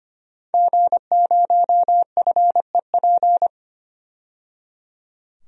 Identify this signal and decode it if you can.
Morse code, heard as a keyed tone, about 25 wpm, 700 Hz. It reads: G0FEP
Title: DXCC-100 Call Signs - 5 Characters - 25wpm